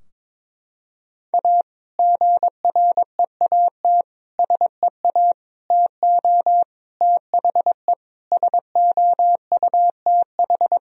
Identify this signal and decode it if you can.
Morse code, heard as a keyed tone, about 21 wpm, 700 Hz. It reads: A GREAT SEA TO THE SOUTH
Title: Sentences from Top 1000 Words - Review Part 2 - 22wpm